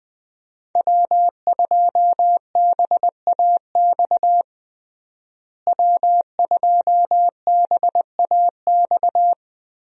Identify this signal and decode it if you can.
Morse code, heard as a keyed tone, about 20 wpm, 700 Hz. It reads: W2BAX W2BAX
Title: USA Call Signs - 5 Characters - Repeated 2x - 20wpm